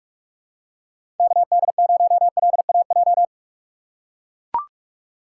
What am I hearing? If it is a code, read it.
Morse code, heard as a keyed tone, about 44 wpm, 690 Hz. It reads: KD0LAJ
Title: USA Call Signs - 4x Character Spacing - 45wpm